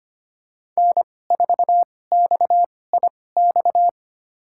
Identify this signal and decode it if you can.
Morse code, heard as a keyed tone, about 25 wpm, spaced as for 18 wpm, 700 Hz. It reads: N4XIX
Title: USA Call Signs - 5 Characters - 2x Character Spacing - 25wpm